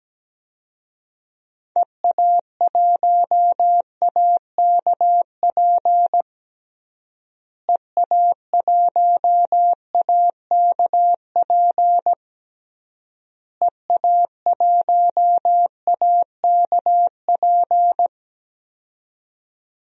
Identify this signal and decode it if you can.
Morse code, heard as a keyed tone, about 17 wpm, 700 Hz. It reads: EA1AKP EA1AKP EA1AKP
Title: DXCC-100 Call Signs - 6 Characters - Repeated 3x - 17wpm